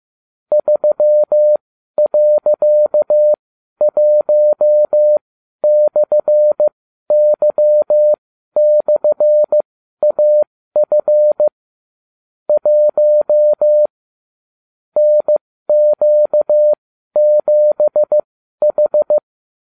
Morse code, heard as a keyed tone, about 15 wpm, spaced as for 12 wpm, 600 Hz. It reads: 3.1/Y/AF 1 NQ7H